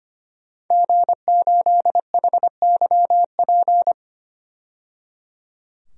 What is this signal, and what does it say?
Morse code, heard as a keyed tone, about 25 wpm, 700 Hz. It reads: G8HYP
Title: DXCC-100 Call Signs - 5 Characters - 25wpm